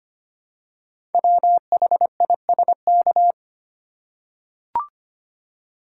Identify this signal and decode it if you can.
Morse code, heard as a keyed tone, about 25 wpm, 690 Hz. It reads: WHISK
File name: practice / 25wpm / Easy Words with Hints - 25wpm